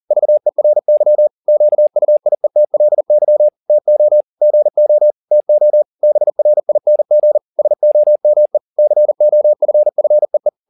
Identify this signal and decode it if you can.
Morse code, heard as a keyed tone, about 40 wpm, 600 Hz. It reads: VERY QUIETLY TO GO TO BRING SOME COFFEE
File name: the old man and the sea 2 40wpm